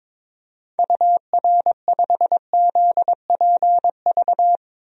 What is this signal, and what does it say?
Morse code, heard as a keyed tone, about 22 wpm, 700 Hz. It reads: UR5ZPV
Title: DXCC-100 Call Signs - 6 Characters - 22wpm